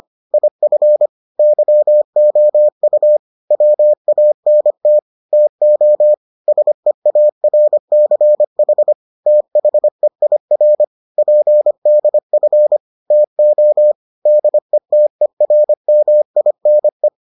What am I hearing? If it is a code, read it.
Morse code, heard as a keyed tone, about 25 wpm, 600 Hz. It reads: IF YOU WANT TO SEARCH THEIR PDF TO DETERMINE